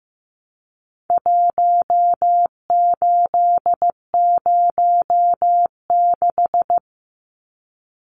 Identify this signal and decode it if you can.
Morse code, heard as a keyed tone, about 15 wpm, 700 Hz. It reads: 1806
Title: Numbers - 4-Digit 15wpm